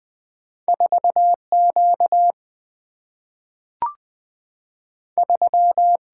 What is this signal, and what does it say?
Morse code, heard as a keyed tone, about 20 wpm, 700 Hz. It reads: 4 Q 3
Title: Single-Letter Single-Number 20wpm